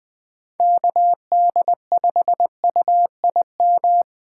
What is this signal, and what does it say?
Morse code, heard as a keyed tone, about 20 wpm, 700 Hz. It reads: KD5UIM
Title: USA Call Signs - 6 Characters - 4x Character Spacing - 20wpm